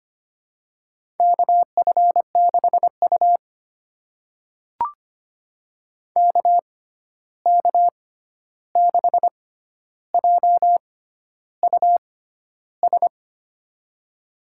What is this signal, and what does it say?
Morse code, heard as a keyed tone, about 25 wpm, spaced as for 11 wpm, 700 Hz. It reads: KF6U KK6JUS
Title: USA Call Signs - 6x Character Spacing - 25wpm